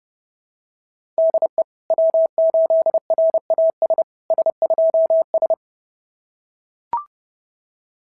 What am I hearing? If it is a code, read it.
Morse code, heard as a keyed tone, about 28 wpm, 650 Hz. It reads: DE W8RAS S2S
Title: SOTA QSO Practice 30wpm